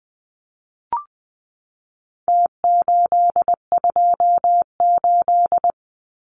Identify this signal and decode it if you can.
Morse code, heard as a keyed tone, about 20 wpm, 700 Hz. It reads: T828
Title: Numbers - Contest Style - 4-Digit 20wpm